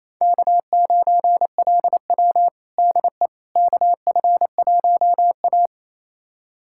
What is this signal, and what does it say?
Morse code, heard as a keyed tone, about 28 wpm, 700 Hz. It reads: K9LW DE KF1A
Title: USA Callsign DE Callsign - 4 Characters - 28wpm